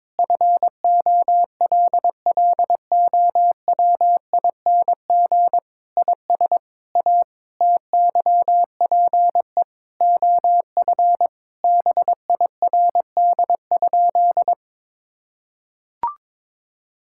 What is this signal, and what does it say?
Morse code, heard as a keyed tone, about 22 wpm, 700 Hz. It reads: FOLLOWING IS A TYPE OF BIRD?
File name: practice / 22wpm / Sentences with 8-Word Phrases - 22wpm